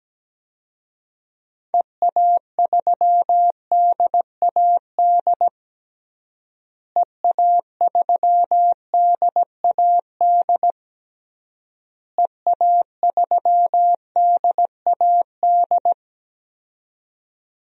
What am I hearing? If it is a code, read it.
Morse code, heard as a keyed tone, about 17 wpm, 700 Hz. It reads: EA3DAD EA3DAD EA3DAD
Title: DXCC-100 Call Signs - 6 Characters - Repeated 3x - 17wpm